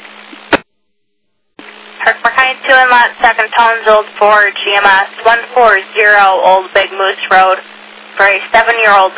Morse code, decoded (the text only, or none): none